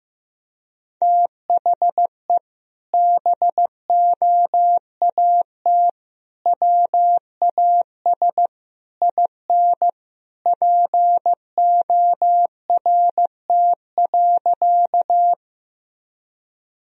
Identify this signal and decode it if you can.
Morse code, heard as a keyed tone, about 15 wpm, 700 Hz. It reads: THE BOAT WAS IN PORT.
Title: Sentences from Top 1000 Words - Review Part 1 - 15wpm